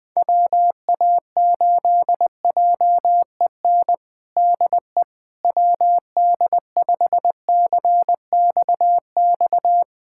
Morse code, heard as a keyed tone, about 20 wpm, 700 Hz. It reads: WA8JEN DE WD5CXX
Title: USA Callsign DE Callsign 20wpm